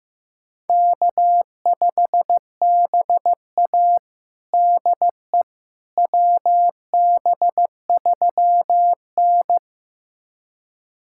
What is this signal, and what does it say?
Morse code, heard as a keyed tone, about 15 wpm, 700 Hz. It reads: K5BA DE WB3N
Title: USA Callsign DE Callsign - 4 Characters - 15wpm